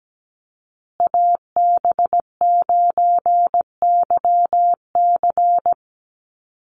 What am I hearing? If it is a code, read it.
Morse code, heard as a keyed tone, about 17 wpm, 700 Hz. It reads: AB9YC
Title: Call Signs Repeated 2x - 17wpm